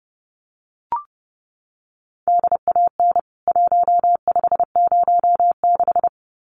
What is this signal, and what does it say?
Morse code, heard as a keyed tone, about 30 wpm, 700 Hz. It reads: DAN 1506
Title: CWT Exchange 30wpm